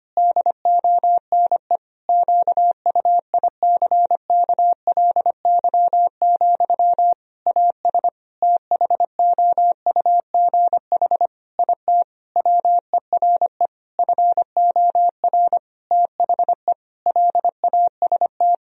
Morse code, heard as a keyed tone, about 25 wpm, 700 Hz. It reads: DONE QUICKLY, AS THOUGH IT WERE FOR THE LAST
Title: Sentences from Top 1000 Words - Review Part 2 - 25wpm